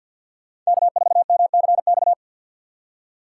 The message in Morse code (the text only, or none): K4MCX